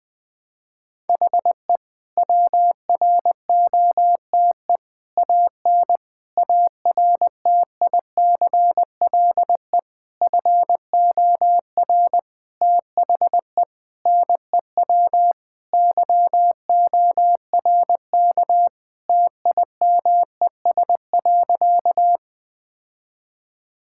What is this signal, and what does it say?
Morse code, heard as a keyed tone, about 20 wpm, 700 Hz. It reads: HE WROTE AN ARTICLE FOR THE NEW YORK TIMES.
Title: Sentences with 5-Word Phrases - Spoken Beginning - 20wpm